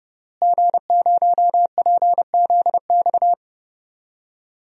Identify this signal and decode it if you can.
Morse code, heard as a keyed tone, about 30 wpm, 700 Hz. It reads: G0PZX
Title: DXCC-100 Call Signs - 5 Characters - Repeated 3x - 30wpm